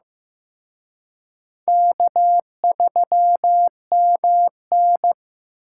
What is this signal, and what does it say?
Morse code, heard as a keyed tone, about 18 wpm, 700 Hz. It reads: K3MN
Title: Speed Racing - USA 1x2 Call Signs 15wpm